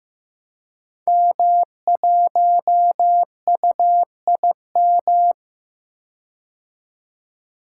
Morse code, heard as a keyed tone, about 15 wpm, 700 Hz. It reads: M1UIM
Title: Code Groups - 5 Letter:Numbers - 15wpm